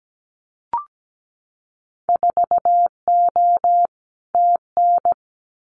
Morse code, heard as a keyed tone, about 17 wpm, 700 Hz. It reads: 4O TN
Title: Winter Field Day Exchanges Encore 17wpm